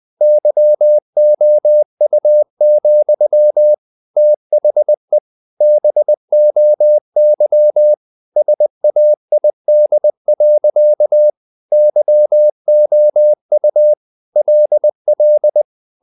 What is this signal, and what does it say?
Morse code, heard as a keyed tone, about 20 wpm, 600 Hz. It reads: YOU, THE BOY SAID. YOU LL
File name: the old man and the sea / the old man and the sea part 1 20wpm